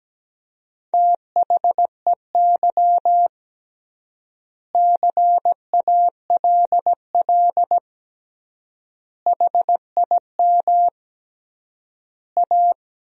Morse code, heard as a keyed tone, about 17 wpm, 700 Hz. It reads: THEY CALL HIM A
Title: 3x Word-Spacing - Sentences from Top 500 Words - 17wpm